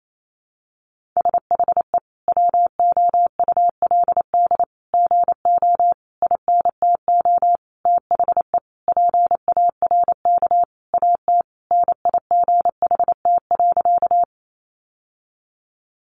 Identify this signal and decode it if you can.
Morse code, heard as a keyed tone, about 28 wpm, 700 Hz. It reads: SHE WOULD GO INTO THE PARK AT NIGHT.
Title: Sentences from Top 1300 Words 28wpm